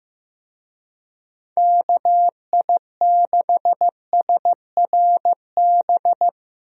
Morse code, heard as a keyed tone, about 16 wpm, 700 Hz. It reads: KI6SRB
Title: USA Call Signs - 6 Characters - Repeated 3x - 15wpm